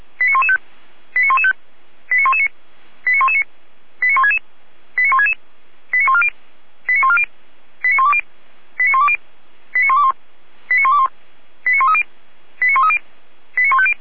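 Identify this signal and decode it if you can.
Five-tone selective call (ZVEI), heard as quick runs of five tones, tones of about 70 ms. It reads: 89115 89115 89119 89119 89166 89166 89140 89140 89120 89120 89121 89121 89130 89130 89150
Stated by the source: ZVEI five-tone selective call received via SDR.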